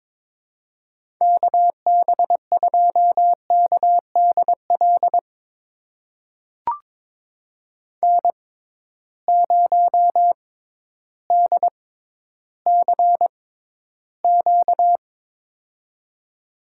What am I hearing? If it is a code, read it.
Morse code, heard as a keyed tone, about 22 wpm, spaced as for 12 wpm, 700 Hz. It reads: KB2KDL N0DCQ
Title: USA Call Signs - 6x Character Spacing - 22wpm